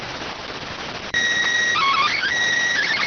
SSTV (slow-scan television), PD 290.